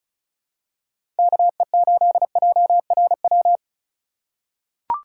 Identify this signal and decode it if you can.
Morse code, heard as a keyed tone, about 36 wpm, 700 Hz. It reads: KE8JRW
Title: USA Call Signs - 6 Characters - 3x Character Spacing - 35wpm